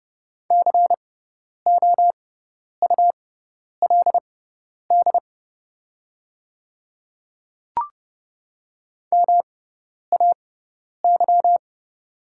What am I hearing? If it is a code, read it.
Morse code, heard as a keyed tone, about 30 wpm, spaced as for 10 wpm, 700 Hz. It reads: COULD MAY
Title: 6x Character Spacing - Top 100 Words - 30wpm